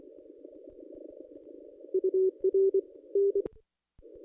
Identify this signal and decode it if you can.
Morse code, heard as a keyed tone, about 24 wpm, 440 Hz. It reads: UR N
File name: contests / ss-cw-k5tr-op-n5rz -1400